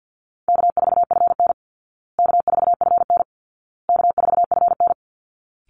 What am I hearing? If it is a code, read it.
Morse code, heard as a keyed tone, about 50 wpm, 700 Hz. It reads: K4FN K4FN K4FN
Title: Call Signs Repeated 2x - 50wpm